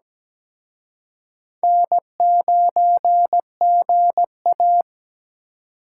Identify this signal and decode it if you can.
Morse code, heard as a keyed tone, about 20 wpm, 700 Hz. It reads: N9GA
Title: Speed Racing - USA 1x2 Call Signs 17wpm